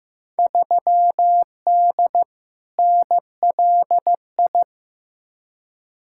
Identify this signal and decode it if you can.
Morse code, heard as a keyed tone, about 15 wpm, 700 Hz. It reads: 3D NLI
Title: Field Day Exchanges Encore - 15wpm